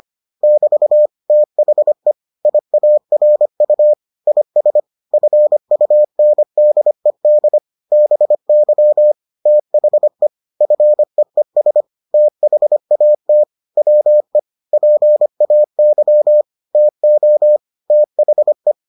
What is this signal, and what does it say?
Morse code, heard as a keyed tone, about 25 wpm, 600 Hz. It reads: <BT> THE IARU IS FUNDED BY THE FEES THAT WE PAY TO THE